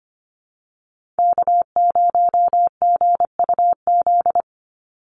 Morse code, heard as a keyed tone, about 25 wpm, 700 Hz. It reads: K0GUZ
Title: Call Signs Repeated 3x - 25wpm